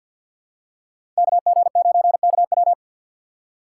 Morse code, heard as a keyed tone, about 47 wpm, 700 Hz. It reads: KG9KW
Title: USA Call Signs - 5 Characters - 4x Character Spacing - 50wpm